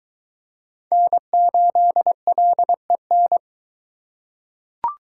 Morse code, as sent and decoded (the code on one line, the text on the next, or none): -. ---.. .-.. -.
N8LEN